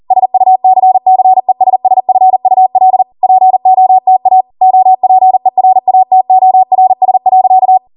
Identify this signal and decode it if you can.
Morse code, heard as a keyed tone, about 40 wpm, 750 Hz. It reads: SUCCESSFUL POTA OPERATORS.